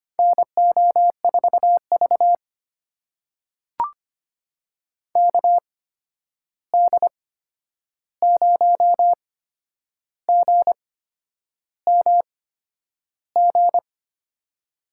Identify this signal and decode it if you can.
Morse code, heard as a keyed tone, about 25 wpm, spaced as for 9 wpm, 700 Hz. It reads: NO4V KD0GMG
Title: USA Call Signs - 8x Character Spacing - 25wpm